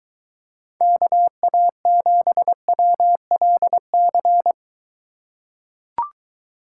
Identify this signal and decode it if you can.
Morse code, heard as a keyed tone, about 23 wpm, 700 Hz. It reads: KA7WLC